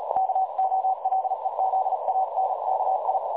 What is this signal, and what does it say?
Morse code, heard as a keyed tone, about 38 wpm, 760 Hz, with noise. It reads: KOLEZANKI
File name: dlazaawansowanych / nagraniazpasma / websdr recording 3565.0kHz